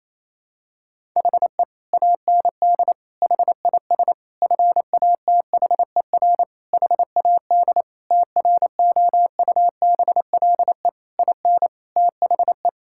Morse code, heard as a keyed tone, about 27 wpm, 700 Hz. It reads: HE AND HIS FATHER HAD TROUBLE IN THE
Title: Sentences from Top 1600 Words 28wpm